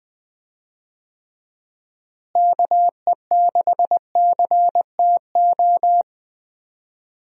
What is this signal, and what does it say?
Morse code, heard as a keyed tone, about 21 wpm, 700 Hz. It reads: KE6CTO
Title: USA Call Signs - 6 Characters - Repeated 2x - 20wpm